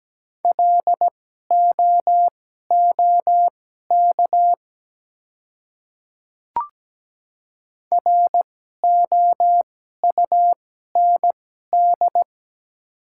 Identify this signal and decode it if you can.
Morse code, heard as a keyed tone, about 17 wpm, spaced as for 12 wpm, 700 Hz. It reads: LOOK ROUND
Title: 2x Character Spacing - Top 300 Words - 17wpm